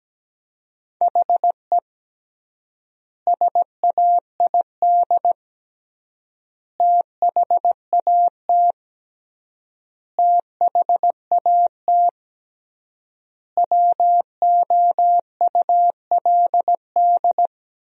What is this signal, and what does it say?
Morse code, heard as a keyed tone, about 17 wpm, 700 Hz. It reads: HE SAID THAT THAT WOULD
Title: 3x Word-Spacing - Sentences from Top 200 Words - 17wpm